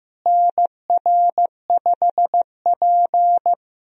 Morse code, heard as a keyed tone, about 15 wpm, 700 Hz. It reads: NR5P
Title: USA Call Signs - 4 Characters - Repeated 3x - 15wpm